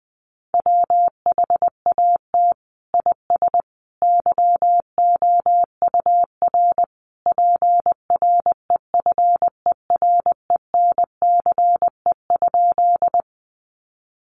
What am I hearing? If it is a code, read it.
Morse code, heard as a keyed tone, about 20 wpm, 700 Hz. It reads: WHAT IS YOUR PREFERENCE?
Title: Sentences from Top 1500 Words 20wpm